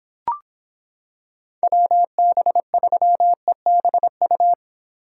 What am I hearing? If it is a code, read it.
Morse code, heard as a keyed tone, about 26 wpm, 700 Hz. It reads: WB3EBU